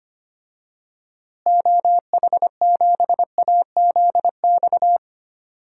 Morse code, heard as a keyed tone, about 25 wpm, 700 Hz. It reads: OH7AZX
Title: DXCC-100 Call Signs - 6 Characters - Repeated 2x - 25wpm